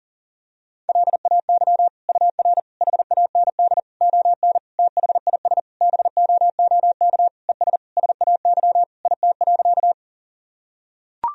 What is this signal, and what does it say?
Morse code, heard as a keyed tone, about 40 wpm, 700 Hz. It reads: LAY UR HAND ON THIS BOOK ES SAY IT.
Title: Sentences from Top 400 Words and Abbreviations 40wpm